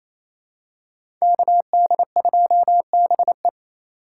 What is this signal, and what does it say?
Morse code, heard as a keyed tone, about 28 wpm, 700 Hz. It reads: KD2BE
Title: USA Call Signs - 5 Characters - 2x Character Spacing - 28wpm